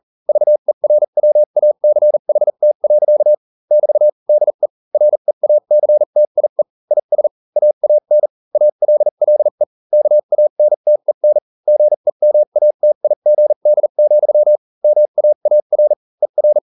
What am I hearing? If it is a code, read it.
Morse code, heard as a keyed tone, about 40 wpm, spaced as for 35 wpm, 600 Hz. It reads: VERWACHT. <BT> DE REACTIE IS AAN ALLE KANTEN GEMATIGD, MAAR ER